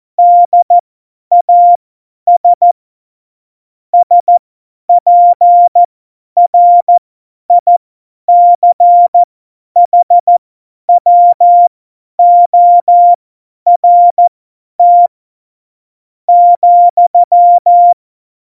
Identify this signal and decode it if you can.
Morse code, heard as a keyed tone, about 14 wpm, spaced as for 9 wpm, 700 Hz. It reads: DAS SPRICHWORT,
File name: Morsekurs / Witze / Witz 04. Februar 2020, 50 F70